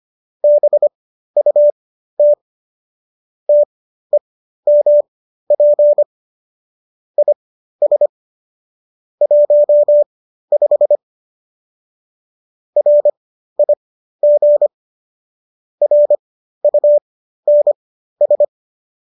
Morse code, heard as a keyed tone, about 25 wpm, spaced as for 13 wpm, 590 Hz. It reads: BUT TEMP IS 15 RIG RUNS